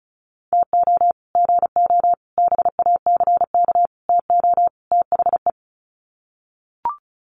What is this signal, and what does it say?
Morse code, heard as a keyed tone, about 34 wpm, 700 Hz. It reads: TO GO BACK TO THE
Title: Sets of 5 Words 35wpm